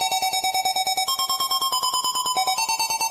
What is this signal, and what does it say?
There is no code.